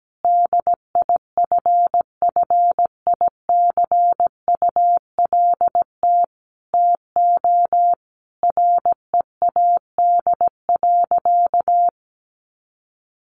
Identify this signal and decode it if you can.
Morse code, heard as a keyed tone, about 17 wpm, 700 Hz. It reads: DIFFICULT TO READ.
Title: Sentences from Top 1400 Words 17wpm